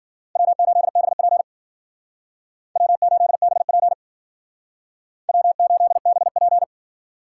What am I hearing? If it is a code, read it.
Morse code, heard as a keyed tone, about 52 wpm, 700 Hz. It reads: W8BP W8BP W8BP